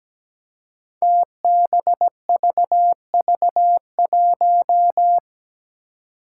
Morse code, heard as a keyed tone, about 17 wpm, 700 Hz. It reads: TBVV1
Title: Five Challenging Letter-Number Combinations 17wpm